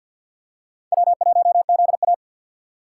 Morse code, heard as a keyed tone, about 53 wpm, 700 Hz. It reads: W1ZA
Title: USA Call Signs - 3x Character Spacing - 50wpm